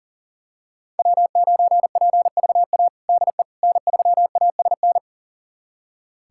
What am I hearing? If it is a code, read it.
Morse code, heard as a keyed tone, about 37 wpm, 700 Hz. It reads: W9PVA DE N3ASN
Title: USA Callsign DE Callsign - 5 Characters - 40wpm